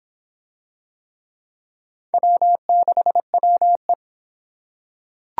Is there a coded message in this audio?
Yes: Morse code from 2.1 to 3.9 s, heard as a keyed tone, about 25 wpm, 700 Hz. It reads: W6WE